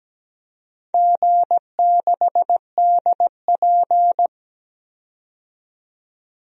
Morse code, heard as a keyed tone, about 17 wpm, 700 Hz. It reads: G6DP
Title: DXCC-100 Call Signs - 4 Characters - 17wpm